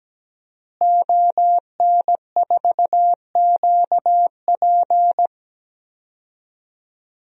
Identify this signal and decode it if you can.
Morse code, heard as a keyed tone, about 17 wpm, 700 Hz. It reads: ON4QP
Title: DXCC-100 Call Signs - 5 Characters - Repeated 2x - 17wpm